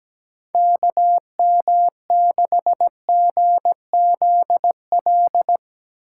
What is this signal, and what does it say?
Morse code, heard as a keyed tone, about 17 wpm, 700 Hz. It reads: KM6GZL
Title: USA Call Signs - 6 Characters - 4x Character Spacing - 17wpm